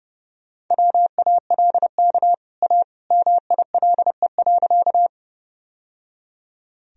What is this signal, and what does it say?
Morse code, heard as a keyed tone, about 30 wpm, 700 Hz. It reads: WALK A MILE.
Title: Sentences from Top 1000 Words - Review Part 1 - 30wpm